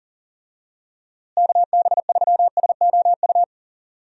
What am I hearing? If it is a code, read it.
Morse code, heard as a keyed tone, about 40 wpm, 700 Hz. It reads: KB3SOU
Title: USA Call Signs - 6 Characters - Repeated 3x - 40wpm